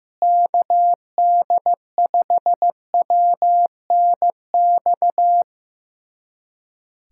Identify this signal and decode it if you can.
Morse code, heard as a keyed tone, about 15 wpm, 700 Hz. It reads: KD5WNX
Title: USA Callsign DE Callsign - 6 Characters - 15wpm